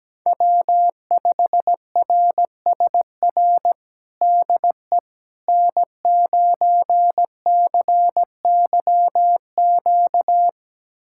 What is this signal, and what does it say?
Morse code, heard as a keyed tone, about 17 wpm, 700 Hz. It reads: W5RSR DE N9CYQ
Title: USA Callsign DE Callsign - 5 Characters - 17wpm